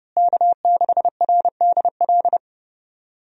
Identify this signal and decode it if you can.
Morse code, heard as a keyed tone, about 32 wpm, 700 Hz. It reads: K6RDL
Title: USA Call Signs - 5 Characters - Repeated 3x - 30wpm